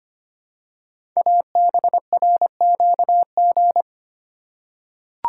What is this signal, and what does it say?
Morse code, heard as a keyed tone, about 25 wpm, 700 Hz. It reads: ABRQG
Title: Code Groups - 5 Letters - 25wpm